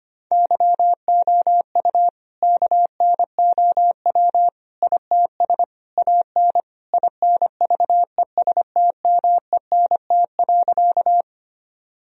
Morse code, heard as a keyed tone, about 25 wpm, 700 Hz. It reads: YOU KNOW ITS AN INVESTMENT.
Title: Sentences from Top 1000 Words - Review Part 4 - 25wpm